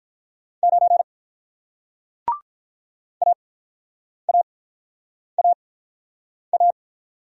Morse code, heard as a keyed tone, about 43 wpm, 700 Hz. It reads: 9 A A A A
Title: Speed-Racing Single Letter-Number 35wpm